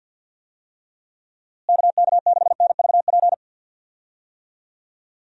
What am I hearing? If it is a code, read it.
Morse code, heard as a keyed tone, about 49 wpm, 700 Hz. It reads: KK6NVP
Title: USA Call Signs - 6 Characters - Repeated 3x - 50wpm